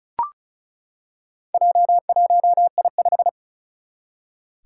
Morse code, heard as a keyed tone, about 36 wpm, 700 Hz. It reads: J1I5
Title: Four Challenging Letter-Number Combinations 35wpm